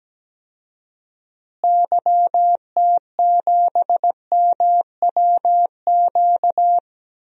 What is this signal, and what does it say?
Morse code, heard as a keyed tone, about 17 wpm, 700 Hz. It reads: YT7MWQ
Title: DXCC-100 Call Signs - 6 Characters - Repeated 3x - 17wpm